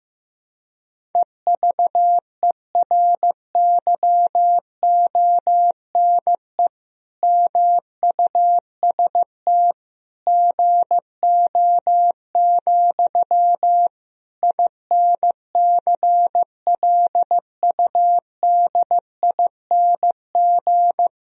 Morse code, heard as a keyed tone, about 15 wpm, 700 Hz. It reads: EVERYONE MUST GO, INCLUDING
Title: Sentences from Top 1000 Words - Review Part 4 - 15wpm